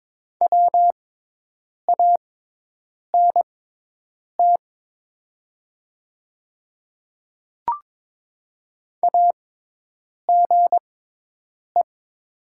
Morse code, heard as a keyed tone, about 22 wpm, spaced as for 8 wpm, 700 Hz. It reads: WANT AGE
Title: 6x Character Spacing - Top 500 Words - 22wpm